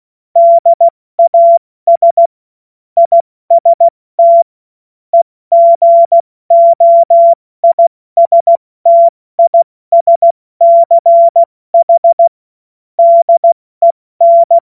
Morse code, heard as a keyed tone, about 16 wpm, 680 Hz. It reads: DAS IST EGOISTISCH DEN
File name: Morsekurs / Witze / Witz 29. Juni 2021, 70 F80